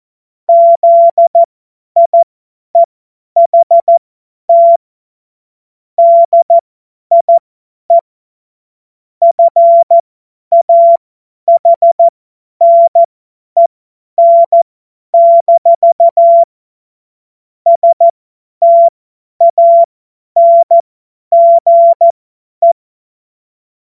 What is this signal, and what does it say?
Morse code, heard as a keyed tone, about 14 wpm, spaced as for 10 wpm, 680 Hz. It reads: ZIEHT DIE FAHNEN- STANGE
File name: Morsekurs / Witze / Witz 21. Juni 2022 50 F70